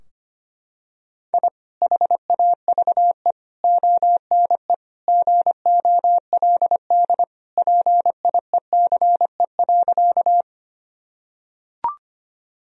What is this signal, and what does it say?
Morse code, heard as a keyed tone, about 25 wpm, 700 Hz. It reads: I HAVE ONE GOLD PIECE.
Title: Sentences from Top 1000 Words - Review Part 3 - 25wpm